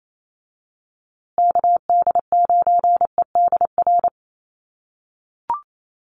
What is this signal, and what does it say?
Morse code, heard as a keyed tone, about 28 wpm, 700 Hz. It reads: KD9EDR
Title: USA Call Signs Repeated 2x 28wpm